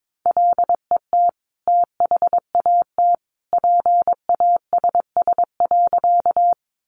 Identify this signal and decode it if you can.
Morse code, heard as a keyed tone, about 22 wpm, 700 Hz. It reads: LET THAT PASS.
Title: Sentences from Top 400 Words 22wpm